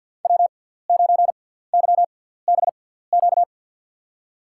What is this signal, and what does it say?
Morse code, heard as a keyed tone, about 50 wpm, spaced as for 17 wpm, 700 Hz. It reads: W9YBQ
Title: USA Call Signs - 5 Characters - 6x Character Spacing - 50wpm